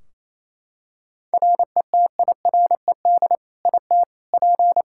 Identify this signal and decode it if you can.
Morse code, heard as a keyed tone, about 28 wpm, 700 Hz. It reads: RETIRED IT P
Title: Practice QSO - Exchange 3 - More Info - 28wpm